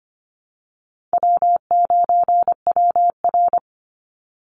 Morse code, heard as a keyed tone, about 25 wpm, 700 Hz. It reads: W9WR
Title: Call Signs Repeated 2x - 25wpm